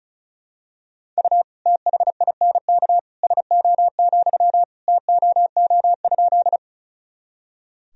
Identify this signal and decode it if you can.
Morse code, heard as a keyed tone, about 36 wpm, 700 Hz. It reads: U THINK SO, TOO?
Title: Sentences from Top 200 Words and Abbreviations 35wpm